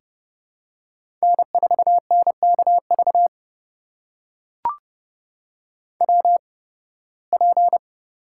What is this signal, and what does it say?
Morse code, heard as a keyed tone, about 30 wpm, spaced as for 20 wpm, 700 Hz. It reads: N4NKV WP
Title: USA Call Signs - 5 Characters - 8x Character Spacing - 30wpm